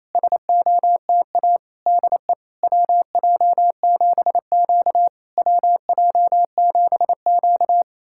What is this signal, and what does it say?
Morse code, heard as a keyed tone, about 27 wpm, 690 Hz. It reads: SOTA DE WJ7Q WJ7Q
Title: SOTA QSO Practice 28wpm